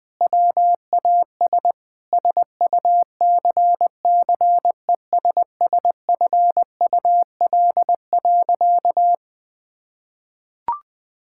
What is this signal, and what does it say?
Morse code, heard as a keyed tone, about 19 wpm, 700 Hz. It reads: WAS SUCCESSFUL.
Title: Sentences with 7-Word Phrases - 20wpm